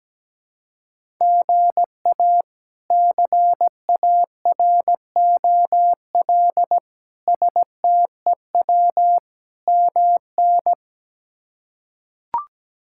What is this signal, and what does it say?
Morse code, heard as a keyed tone, about 17 wpm, 700 Hz. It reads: GA CAROL STEW MN
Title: K1USN SST QSO Practice - 17wpm